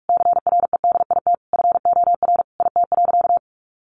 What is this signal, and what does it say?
Morse code, heard as a keyed tone, about 46 wpm, 700 Hz. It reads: CREDIT FOR IT.
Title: Sentences from Top 1000 Words 45wpm